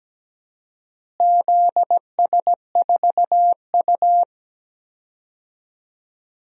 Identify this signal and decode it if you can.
Morse code, heard as a keyed tone, about 17 wpm, 700 Hz. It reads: ZS4U
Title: DXCC-100 Call Signs - 4 Characters - Repeated 2x - 17wpm